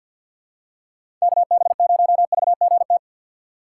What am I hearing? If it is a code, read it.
Morse code, heard as a keyed tone, about 53 wpm, 700 Hz. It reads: KB0VGT
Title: USA Call Signs - 6 Characters - 4x Character Spacing - 50wpm